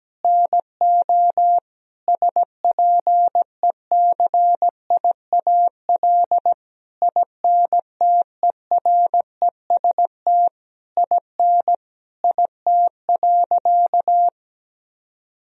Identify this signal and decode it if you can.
Morse code, heard as a keyed tone, about 17 wpm, 700 Hz. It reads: NO SPECIAL INTEREST IN IT.
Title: Sentences from Top 1000 Words - Review Part 3 - 17wpm